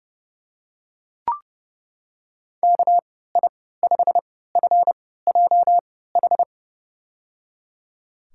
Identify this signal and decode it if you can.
Morse code, heard as a keyed tone, about 30 wpm, spaced as for 17 wpm, 700 Hz. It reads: KI5FJH
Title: USA Call Signs - 6 Characters - 3x Character Spacing - 30wpm